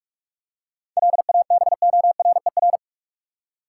Morse code, heard as a keyed tone, about 46 wpm, 700 Hz. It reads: LABORER
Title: Practice QSO - Exchange 3 - More Info - 45wpm